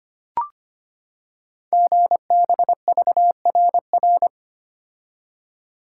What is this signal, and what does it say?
Morse code, heard as a keyed tone, about 25 wpm, 700 Hz. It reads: GBVRR
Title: Code Groups - 5 Letters - 25wpm